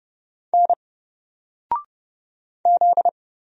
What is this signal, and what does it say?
Morse code, heard as a keyed tone, about 30 wpm, 700 Hz. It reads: N Z
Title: Single Letter-Number Alt 30wpm